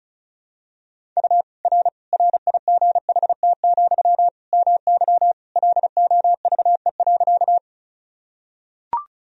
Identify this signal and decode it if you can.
Morse code, heard as a keyed tone, about 36 wpm, 700 Hz. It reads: U R RIGHT, MY LOVE.
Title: Sentences from Top 400 Words and Abbreviations 35wpm